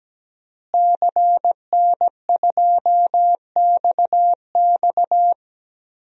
Morse code, heard as a keyed tone, about 17 wpm, 700 Hz. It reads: CN2XX
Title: DXCC-100 Call Signs - 5 Characters - Repeated 3x - 17wpm